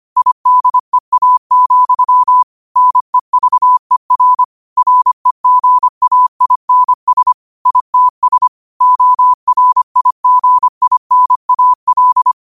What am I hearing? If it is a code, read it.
Morse code, heard as a keyed tone, about 25 wpm, 1000 Hz. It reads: IDEA, NEVER REGAINS ITS ORIGINAL